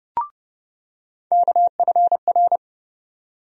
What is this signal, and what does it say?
Morse code, heard as a keyed tone, about 32 wpm, 700 Hz. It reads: KFR
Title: USA Call Sign Postfixes - 3 Letters 30wpm